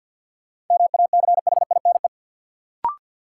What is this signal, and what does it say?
Morse code, heard as a keyed tone, about 48 wpm, 700 Hz. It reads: MACHINE
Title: Sentences from Top 1000 Words - Review Part 2 - New Word Drill - 50wpm